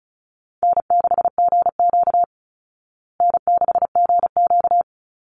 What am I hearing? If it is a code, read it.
Morse code, heard as a keyed tone, about 34 wpm, 700 Hz. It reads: N6GQ N6GQ
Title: Easy Call Signs Repeated 3x - 35wpm